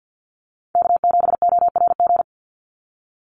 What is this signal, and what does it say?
Morse code, heard as a keyed tone, about 50 wpm, 700 Hz. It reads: K7ORG
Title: NA SOTA Chaser Callsigns 50wpm